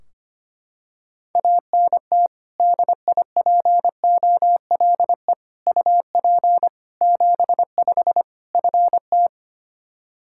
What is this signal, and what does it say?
Morse code, heard as a keyed tone, about 25 wpm, 700 Hz. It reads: ANT DIPOLE UP 75 FT
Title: Practice QSO - Exchange 2 - Info - 25wpm